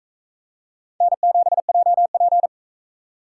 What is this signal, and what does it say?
Morse code, heard as a keyed tone, about 40 wpm, 700 Hz. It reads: N7JP